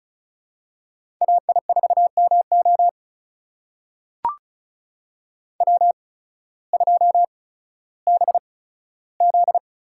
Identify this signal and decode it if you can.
Morse code, heard as a keyed tone, about 36 wpm, spaced as for 16 wpm, 700 Hz. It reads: AI4MO W2BZ
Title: USA Call Signs - 8x Character Spacing - 35wpm